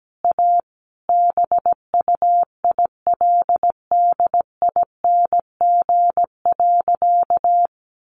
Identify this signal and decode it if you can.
Morse code, heard as a keyed tone, about 17 wpm, 700 Hz. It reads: A BUILDING.
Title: Additional Easy Phrases 17wpm